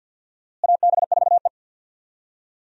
Morse code, heard as a keyed tone, about 52 wpm, 700 Hz. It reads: AB4E
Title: USA Call Signs - 4 Characters - Repeated 3x - 50wpm